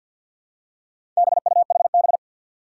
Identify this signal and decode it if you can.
Morse code, heard as a keyed tone, about 51 wpm, 700 Hz. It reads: BUHB
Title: Four Challenging Letter-Number Combinations 50wpm